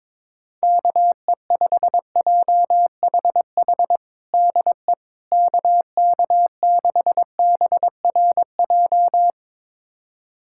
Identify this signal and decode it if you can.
Morse code, heard as a keyed tone, about 22 wpm, 700 Hz. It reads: KE5JHH DE KK6BRJ
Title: USA Callsign DE Callsign - 6 Characters - 22wpm